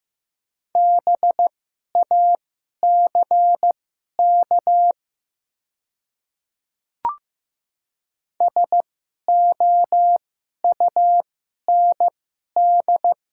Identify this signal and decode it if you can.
Morse code, heard as a keyed tone, about 15 wpm, spaced as for 11 wpm, 700 Hz. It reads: BACK SOUND
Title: 2x Character Spacing - Top 200 Words - 15wpm